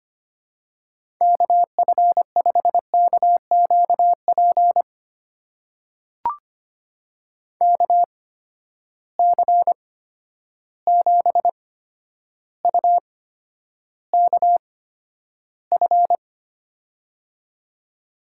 Morse code, heard as a keyed tone, about 25 wpm, spaced as for 11 wpm, 700 Hz. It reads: KF5KQP KC7UKF
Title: USA Call Signs - 6 Characters - 8x Character Spacing - 25wpm